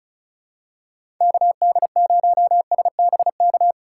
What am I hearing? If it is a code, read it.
Morse code, heard as a keyed tone, about 34 wpm, 700 Hz. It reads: KD0SBK
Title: USA Call Signs - 6 Characters - 2x Character Spacing - 35wpm